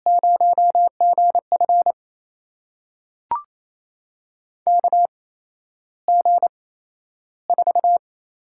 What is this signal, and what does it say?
Morse code, heard as a keyed tone, about 28 wpm, spaced as for 12 wpm, 700 Hz. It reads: 0GF KG4
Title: USA Call Signs - 4 Characters - 8x Character Spacing - 28wpm